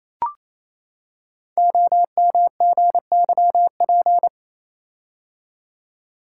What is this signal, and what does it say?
Morse code, heard as a keyed tone, about 27 wpm, 700 Hz. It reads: OMGYP
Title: Code Groups - 5 Letters - 28wpm